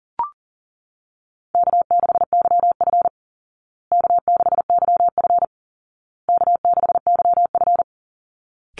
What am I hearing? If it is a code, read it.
Morse code, heard as a keyed tone, about 40 wpm, 700 Hz. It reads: K6YF K6YF K6YF
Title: Easy Call Signs Repeated 3x - 40wpm